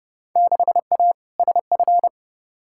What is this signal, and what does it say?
Morse code, heard as a keyed tone, about 32 wpm, 700 Hz. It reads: 6A SF
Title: Field Day Exchanges - 30wpm